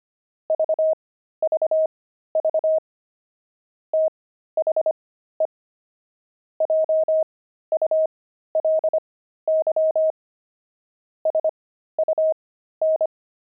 Morse code, heard as a keyed tone, about 25 wpm, spaced as for 13 wpm, 630 Hz. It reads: VVV THE JULY SUN
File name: pangram-fs-13-cs-25-fragment